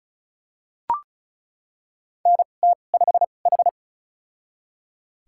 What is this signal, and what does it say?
Morse code, heard as a keyed tone, about 36 wpm, spaced as for 25 wpm, 700 Hz. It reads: NT5H
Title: USA Call Signs - 4 Characters - 2x Character Spacing - 35wpm